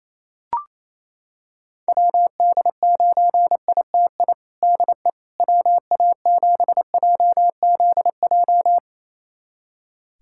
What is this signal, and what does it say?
Morse code, heard as a keyed tone, about 28 wpm, 700 Hz. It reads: WD9ITI DE WA7JZJ
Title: USA Callsign DE Callsign - 6 Characters - 28wpm